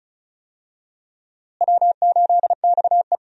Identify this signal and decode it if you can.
Morse code, heard as a keyed tone, about 34 wpm, 700 Hz. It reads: W8XE
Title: USA Call Signs - 4 Characters - Repeated 2x - 35wpm